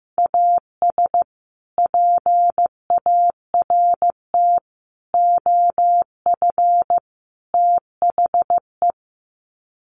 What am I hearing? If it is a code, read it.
Morse code, heard as a keyed tone, about 15 wpm, 700 Hz. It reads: AS PART OF THE
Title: Sets of 4 Words 15wpm